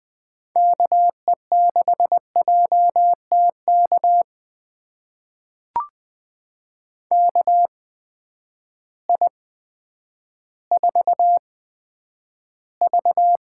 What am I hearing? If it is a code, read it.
Morse code, heard as a keyed tone, about 20 wpm, spaced as for 10 wpm, 700 Hz. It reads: KE6JTK KI4V
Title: USA Call Signs - 6 Characters - 8x Character Spacing - 20wpm